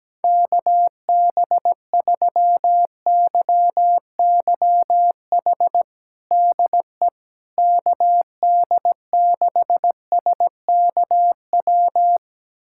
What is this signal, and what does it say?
Morse code, heard as a keyed tone, about 17 wpm, 700 Hz. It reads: KB3YYH DE KD6SKW
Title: USA Callsign DE Callsign - 6 Characters - 17wpm